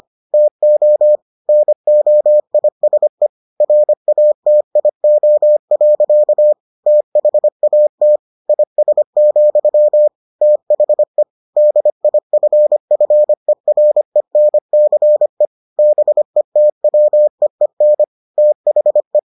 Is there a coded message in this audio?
Yes: Morse code from 0.3 to 19.2 s, heard as a keyed tone, about 25 wpm, 600 Hz. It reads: TO NOISE RATIO. THAT IS, THE DIFFERENCE BETWEEN THE